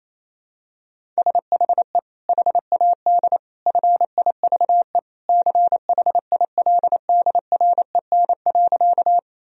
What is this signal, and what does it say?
Morse code, heard as a keyed tone, about 28 wpm, 700 Hz. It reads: SHE HAD FIVE CHILDREN.
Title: Sentences from Top 1000 Words - Review Part 2 - 28wpm